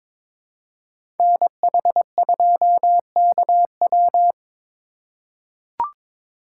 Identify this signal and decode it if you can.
Morse code, heard as a keyed tone, about 22 wpm, 700 Hz. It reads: NH2KW
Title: USA Call Signs - 4x Character Spacing - 22wpm